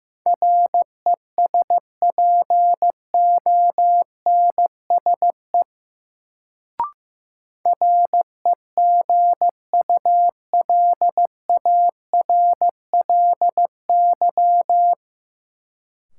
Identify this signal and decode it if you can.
Morse code, heard as a keyed tone, about 15 wpm, 700 Hz. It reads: RESPONSE REGULARLY
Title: Sentences from Top 1400 Words - New Word Drill Alt - 15wpm